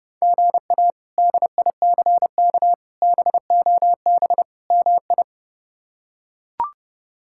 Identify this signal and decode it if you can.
Morse code, heard as a keyed tone, about 30 wpm, 700 Hz. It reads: GA DICK BOB MI
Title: K1USN SST QSO Practice - 30wpm